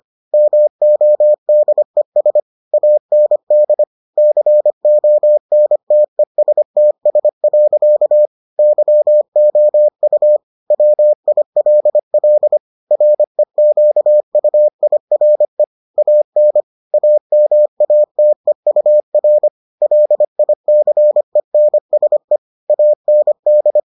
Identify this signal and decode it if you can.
Morse code, heard as a keyed tone, about 25 wpm, 600 Hz. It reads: MODES AND CONTESTS. YOU WILL REQUIRE AN AMATEUR LICENSE AND